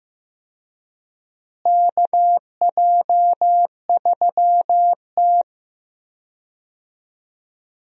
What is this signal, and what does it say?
Morse code, heard as a keyed tone, about 15 wpm, 700 Hz. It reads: KJ3T
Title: USA Call Signs - 4 Characters - Repeated 3x - 15wpm